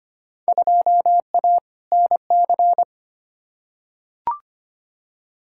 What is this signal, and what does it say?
Morse code, heard as a keyed tone, about 25 wpm, 700 Hz. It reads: 2A NC
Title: Field Day Exchanges - 25wpm